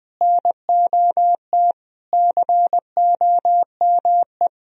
Morse code, heard as a keyed tone, about 19 wpm, 700 Hz. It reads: NOT COME
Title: Sentences from Top 1000 Words - Review Part 1 - 20wpm